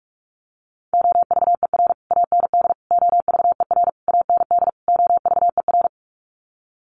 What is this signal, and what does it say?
Morse code, heard as a keyed tone, about 44 wpm, 700 Hz. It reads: OVER AND OVER AND OVER
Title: Sets of 5 Words Encore 45wpm